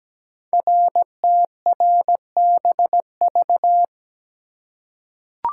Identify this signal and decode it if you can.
Morse code, heard as a keyed tone, about 17 wpm, 700 Hz. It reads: RTRBV
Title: Code Groups - 5 Letters - 17wpm